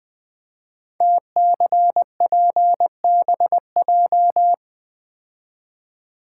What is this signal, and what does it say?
Morse code, heard as a keyed tone, about 20 wpm, 700 Hz. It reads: TCPBJ
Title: Code Groups - 5 Letters - 20wpm